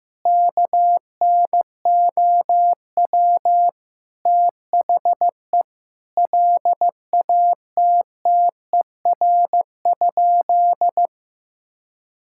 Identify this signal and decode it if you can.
Morse code, heard as a keyed tone, about 15 wpm, 700 Hz. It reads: KNOW THE LATTER?
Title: Sentences from Top 1600 Words 15wpm